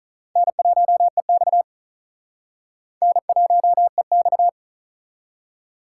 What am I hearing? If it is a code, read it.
Morse code, heard as a keyed tone, about 47 wpm, 700 Hz. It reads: N1EX N1EX